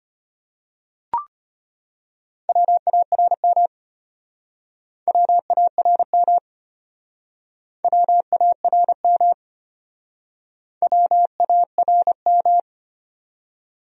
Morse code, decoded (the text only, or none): WARM WARM WARM WARM